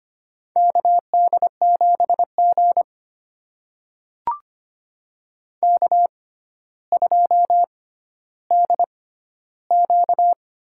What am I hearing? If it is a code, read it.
Morse code, heard as a keyed tone, about 25 wpm, spaced as for 13 wpm, 700 Hz. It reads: KD7G K2DQ
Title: USA Call Signs - 6x Character Spacing - 25wpm